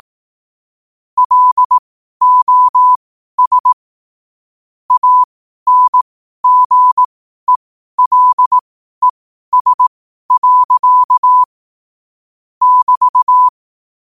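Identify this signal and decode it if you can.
Morse code, heard as a keyed tone, about 18 wpm, spaced as for 12 wpm, 1000 Hz. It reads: LOS ANGELES. <BT>
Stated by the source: Morse at 12 wpm.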